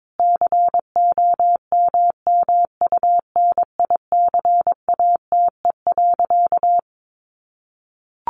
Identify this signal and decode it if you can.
Morse code, heard as a keyed tone, about 22 wpm, 700 Hz. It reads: COMMUNICATE.
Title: Sentences from Top 1300 Words 22wpm